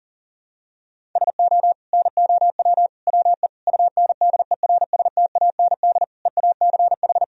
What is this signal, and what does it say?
Morse code, heard as a keyed tone, about 40 wpm, 700 Hz. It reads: SO NOW WE UNDERSTAND EACH
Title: Sentences from Top 1000 Words - Review Part 1 - 40wpm